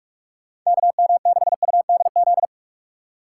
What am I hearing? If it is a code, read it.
Morse code, heard as a keyed tone, about 46 wpm, 700 Hz. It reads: KM6UDZ
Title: USA Call Signs - 6 Characters - 2x Character Spacing - 45wpm